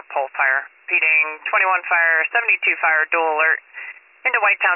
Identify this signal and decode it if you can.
Emergency Alert System (SAME) tones, heard as harsh buzzing bursts, with speech. It is heard from 0.3 to 4.1 s.